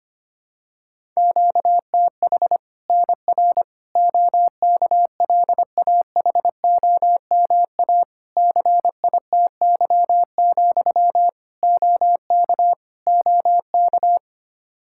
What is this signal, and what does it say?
Morse code, heard as a keyed tone, about 25 wpm, 700 Hz. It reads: QTH NR OKLAHOMA CITY, OK OK
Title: Practice QSO - Exchange 1 - Beginning - 25wpm